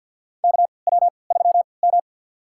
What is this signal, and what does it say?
Morse code, heard as a keyed tone, about 45 wpm, spaced as for 27 wpm, 700 Hz. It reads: KW3M
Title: USA Call Signs - 4 Characters - 3x Character Spacing - 50wpm